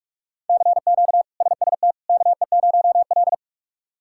Morse code, heard as a keyed tone, about 44 wpm, 700 Hz. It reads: CQ SST KE0L
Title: K1USN SST QSO Practice - 45wpm